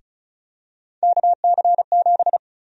Morse code, heard as a keyed tone, about 36 wpm, 700 Hz. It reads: KC7
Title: USA Call Sign Prefixes - 35wpm